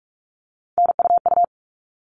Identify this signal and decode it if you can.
Morse code, heard as a keyed tone, about 44 wpm, 700 Hz. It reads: NUU
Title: All Cut Numbers - 3-Digit 45wpm